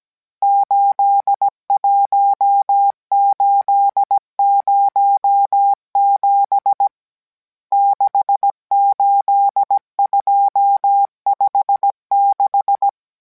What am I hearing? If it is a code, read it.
Morse code, heard as a keyed tone, about 17 wpm, 800 Hz. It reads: 81807 68256